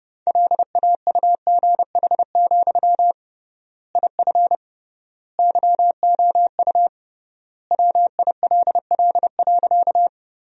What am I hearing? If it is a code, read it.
Morse code, heard as a keyed tone, about 30 wpm, 700 Hz. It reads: LAUGH, IF YOU WILL.
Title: 3x Word-Spacing - Sentences from Top 500 Words - 30wpm